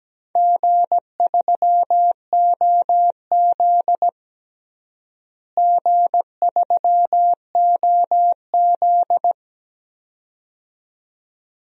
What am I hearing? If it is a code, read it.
Morse code, heard as a keyed tone, about 17 wpm, 700 Hz. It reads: G3OZ G3OZ
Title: DXCC-100 Call Signs - 4 Characters - Repeated 2x - 17wpm